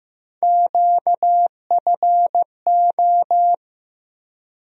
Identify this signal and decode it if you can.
Morse code, heard as a keyed tone, about 15 wpm, 700 Hz. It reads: QFO
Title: USA Call Sign Postfixes - 3 Letters 15wpm